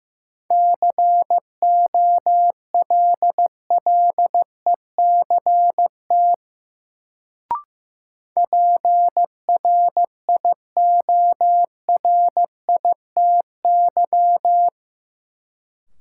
Morse code, heard as a keyed tone, about 15 wpm, 700 Hz. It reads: COLLECT PRIORITY
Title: Sentences from Top 1600 Words - New Word Drill Alt - 15wpm